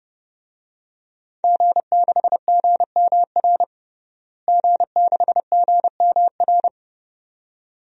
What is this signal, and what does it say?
Morse code, heard as a keyed tone, about 30 wpm, 700 Hz. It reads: G6GMR G6GMR
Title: DXCC-100 Call Signs - 5 Characters - Repeated 2x - 30wpm